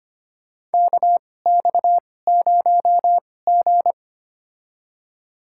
Morse code, heard as a keyed tone, about 25 wpm, spaced as for 18 wpm, 700 Hz. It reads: KX0G
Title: USA Call Signs - 2x Character Spacing - 25wpm